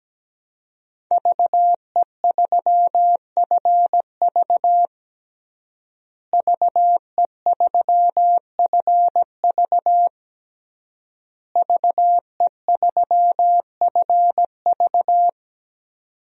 Morse code, heard as a keyed tone, about 17 wpm, 700 Hz. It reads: VE3FV VE3FV VE3FV
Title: DXCC-100 Call Signs - 5 Characters - Repeated 3x - 17wpm